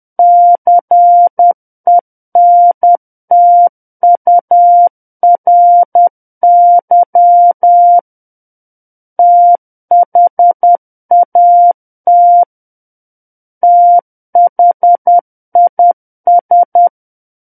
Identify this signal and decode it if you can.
Morse code, heard as a keyed tone, about 10 wpm, 700 Hz. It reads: CENTURY THAT THIS
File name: Book One / 01-Chapter 1 - 10 WPM